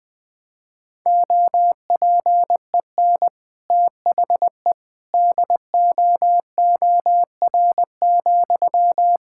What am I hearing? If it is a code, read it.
Morse code, heard as a keyed tone, about 20 wpm, 700 Hz. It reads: OPEN THE DOOR,
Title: Sentences from Top 1000 Words - Review Part 6 - 20wpm